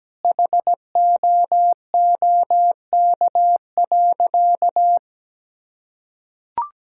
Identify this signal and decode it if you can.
Morse code, heard as a keyed tone, about 17 wpm, 700 Hz. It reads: HOOK.
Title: Sentences from Top 1600 Words 17wpm